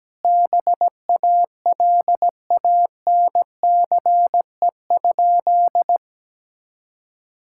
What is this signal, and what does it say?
Morse code, heard as a keyed tone, about 17 wpm, 700 Hz. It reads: BALANCE?
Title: Sentences from Top 1000 Words - Review Part 6 - 17wpm